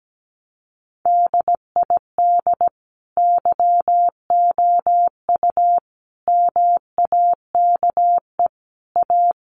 Morse code, heard as a keyed tone, about 17 wpm, 700 Hz. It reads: DID YOU MAKE A
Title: Sentences from Top 1300 Words 17wpm